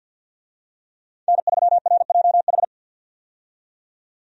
Morse code, heard as a keyed tone, about 46 wpm, 700 Hz. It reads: N3RJH
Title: USA Call Signs - 5 Characters - 50wpm